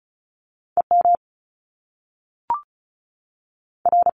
Morse code, heard as a keyed tone, about 36 wpm, spaced as for 6 wpm, 700 Hz. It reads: EMR
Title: Top Two Letter Combinations in Words 35wpm